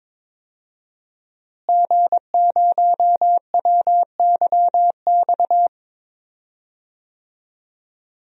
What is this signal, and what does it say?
Morse code, heard as a keyed tone, about 22 wpm, 700 Hz. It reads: G0WYX
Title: DXCC-100 Call Signs - 5 Characters - Repeated 2x - 22wpm